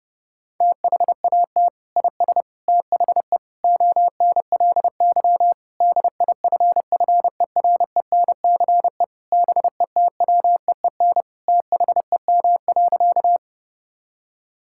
Morse code, heard as a keyed tone, about 30 wpm, 700 Hz. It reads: THAT IS THE ONLY DIFFERENCE BETWEEN THEM.
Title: Sentences from Top 1000 Words - Review Part 5 - 30wpm